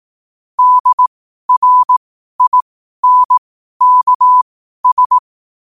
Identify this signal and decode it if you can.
Morse code, heard as a keyed tone, about 18 wpm, spaced as for 12 wpm, 1000 Hz. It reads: DRINKS
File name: qotdfiles / 12wpm